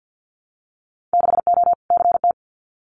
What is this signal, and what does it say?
Morse code, heard as a keyed tone, about 50 wpm, 700 Hz. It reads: 6O CT
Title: Winter Field Day Exchanges Encore 50wpm